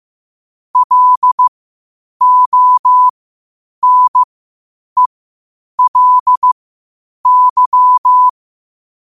Morse code, heard as a keyed tone, about 15 wpm, spaced as for 8 wpm, 1000 Hz. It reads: LONELY
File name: qotdfiles / 8wpm